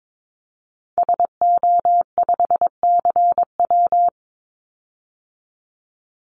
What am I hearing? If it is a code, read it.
Morse code, heard as a keyed tone, about 22 wpm, 700 Hz. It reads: SO5CW
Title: Call Signs - 22wpm